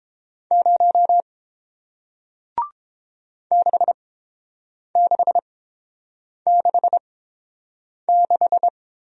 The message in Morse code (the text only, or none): 0 6666